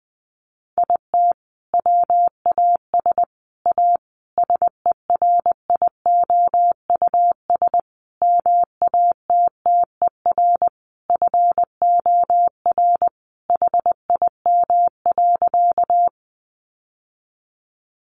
Morse code, heard as a keyed tone, about 20 wpm, 700 Hz. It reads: IT WAS A SERIOUS MATTER FOR HIM.
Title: Sentences from Top 1100 Words 20wpm